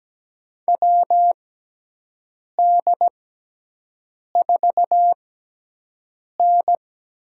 Morse code, heard as a keyed tone, about 17 wpm, 700 Hz. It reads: W D 4 N
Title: USA Call Signs - 6 Characters - 6x Character Spacing - 17wpm